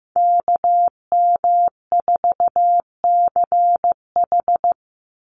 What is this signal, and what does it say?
Morse code, heard as a keyed tone, about 15 wpm, 700 Hz. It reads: KM4CH
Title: Call Signs Repeated 2x - 15wpm